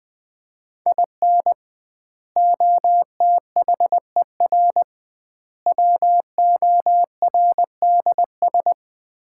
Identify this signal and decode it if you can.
Morse code, heard as a keyed tone, about 20 wpm, 700 Hz. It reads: IN OTHER WORDS
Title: 2x Word-Spacing - Sets of 3 Words - 20wpm